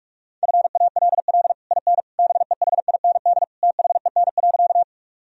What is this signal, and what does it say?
Morse code, heard as a keyed tone, about 43 wpm, 700 Hz. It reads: FALL IN BEHIND THEN.
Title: Sentences from Top 1000 Words - Review Part 2 - 45wpm